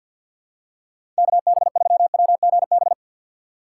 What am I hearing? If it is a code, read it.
Morse code, heard as a keyed tone, about 47 wpm, 700 Hz. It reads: KB3WGB
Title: USA Call Signs - 6 Characters - 4x Character Spacing - 50wpm